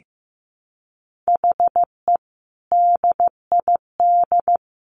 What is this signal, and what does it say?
Morse code, heard as a keyed tone, about 15 wpm, 700 Hz. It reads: HE DID
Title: Sentences from Top 1300 Words 15wpm